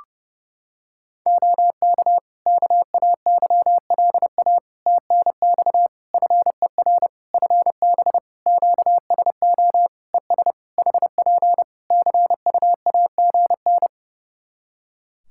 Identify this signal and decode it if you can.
Morse code, heard as a keyed tone, about 30 wpm, 700 Hz. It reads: OK KAYLA TNX FER FB QSO ES HP CUAGN
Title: Practice QSO - Exchange 4 - Ending - 30wpm